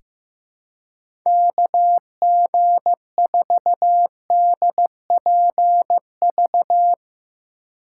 Morse code, heard as a keyed tone, about 16 wpm, 700 Hz. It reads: KG4DPV
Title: USA Call Signs - 6 Characters - 4x Character Spacing - 15wpm